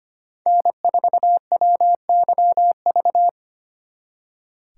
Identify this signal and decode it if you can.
Morse code, heard as a keyed tone, about 30 wpm, 700 Hz. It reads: N4WYV
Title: Speed Racing - USA Call Signs 25wpm